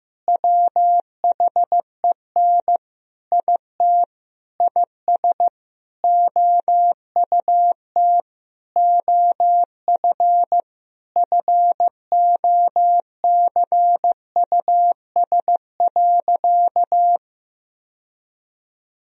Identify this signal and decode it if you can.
Morse code, heard as a keyed tone, about 16 wpm, 700 Hz. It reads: WHEN IT IS OUT OF FOCUS.
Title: Sentences from Top 1000 Words - Review Part 4 - 15wpm